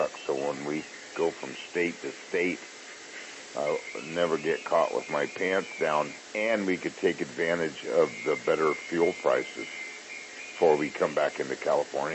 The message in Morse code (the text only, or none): none